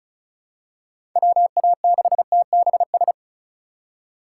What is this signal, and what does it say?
Morse code, heard as a keyed tone, about 34 wpm, 700 Hz. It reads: WA6TBS
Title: USA Call Signs - 4x Character Spacing - 35wpm